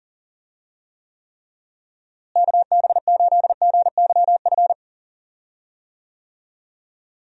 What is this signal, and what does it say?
Morse code, heard as a keyed tone, about 40 wpm, 700 Hz. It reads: KB8GYF
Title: USA Call Signs - 6 Characters - Repeated 3x - 40wpm